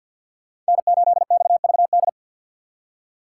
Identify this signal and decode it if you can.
Morse code, heard as a keyed tone, about 64 wpm, 700 Hz. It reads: N8XVD